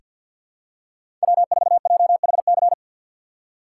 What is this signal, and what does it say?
Morse code, heard as a keyed tone, about 49 wpm, 700 Hz. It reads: W4JHC
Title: USA Call Signs - 5 Characters - 4x Character Spacing - 50wpm